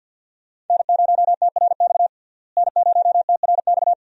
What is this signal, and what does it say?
Morse code, heard as a keyed tone, about 52 wpm, 700 Hz. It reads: N0TRX N0TRX
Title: USA Call Signs - 5 Characters - Repeated 2x - 50wpm